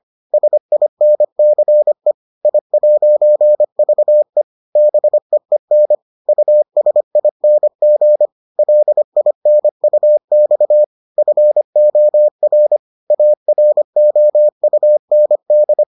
Morse code, heard as a keyed tone, about 25 wpm, 600 Hz. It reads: SINCE I'VE BEEN USING LINUX FOR AROUND